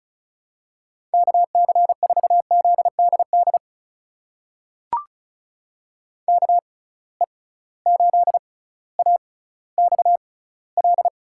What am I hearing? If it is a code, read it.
Morse code, heard as a keyed tone, about 36 wpm, spaced as for 18 wpm, 700 Hz. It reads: KC4ZDD KE8AXL
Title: USA Call Signs - 6 Characters - 6x Character Spacing - 35wpm